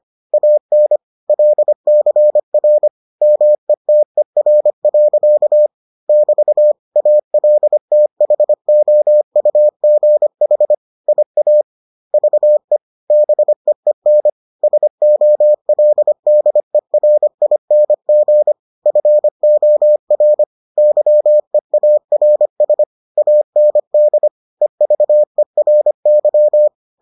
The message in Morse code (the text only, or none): AN LCR METER. <BT> ALTHOUGH IAVE BEEN SOLDERING FOR YEARS AND EVERY